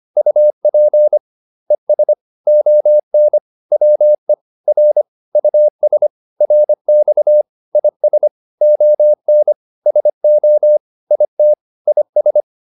Morse code, heard as a keyed tone, about 25 wpm, 600 Hz. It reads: UP ES ON WE R US RX IS ON SO IT IS